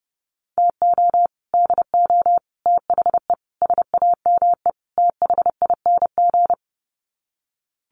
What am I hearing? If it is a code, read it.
Morse code, heard as a keyed tone, about 30 wpm, 700 Hz. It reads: TO DO THE SAME THING
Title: Sets of 5 Words Encore 30wpm